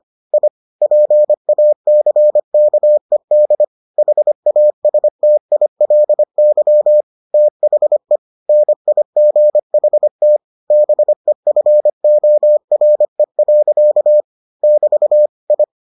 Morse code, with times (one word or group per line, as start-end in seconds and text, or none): 0.3-0.5 I
0.8-3.6 PACKED
4.0-7.0 HASTILY
7.3-8.2 THE
8.5-10.4 NIGHT
10.7-14.2 BEFORE.
14.6-15.3 <BT>
15.5-15.6 I